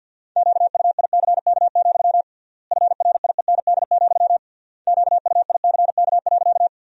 Morse code, heard as a keyed tone, about 52 wpm, 700 Hz. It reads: QUICK, FRIEND, QUICK.
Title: 3x Word-Spacing - Sentences from Top 500 Words - 50wpm